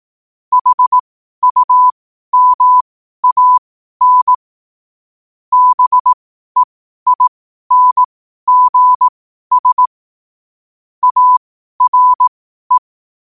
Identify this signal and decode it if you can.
Morse code, heard as a keyed tone, about 18 wpm, spaced as for 12 wpm, 1000 Hz. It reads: HUMAN BEINGS ARE
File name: qotdfiles / 12wpm